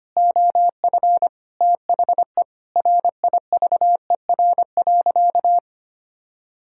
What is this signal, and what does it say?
Morse code, heard as a keyed tone, about 25 wpm, 700 Hz. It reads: OF THE RIVER.
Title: Sentences from Top 1000 Words - Review Part 6 - 25wpm